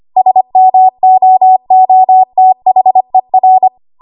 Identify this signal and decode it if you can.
Morse code, heard as a keyed tone, about 25 wpm, 750 Hz. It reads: SMOOTHER
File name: Archive / 25 WPM / 25WPM